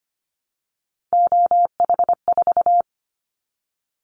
Morse code, heard as a keyed tone, about 25 wpm, 700 Hz. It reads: OH4
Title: Callsign Prefixes for Top 25 DXCC - Repeated 3x - 25wpm